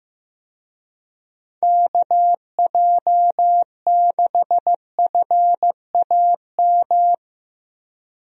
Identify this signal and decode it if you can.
Morse code, heard as a keyed tone, about 15 wpm, 700 Hz. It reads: KJ6FAM
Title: USA Call Signs - 6 Characters - Repeated 2x - 15wpm